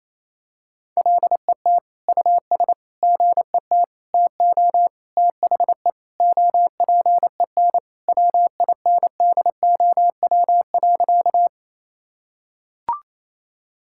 Morse code, heard as a keyed tone, about 28 wpm, 700 Hz. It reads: LET US GET TO THE OPEN WINDOW.
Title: Sentences from Top 1600 Words 28wpm